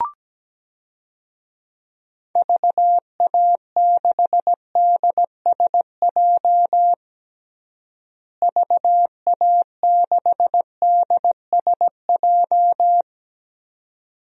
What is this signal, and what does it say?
Morse code, heard as a keyed tone, about 17 wpm, 700 Hz. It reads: VA6DSJ VA6DSJ
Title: DXCC-100 Call Signs - 6 Characters - Repeated 2x - 17wpm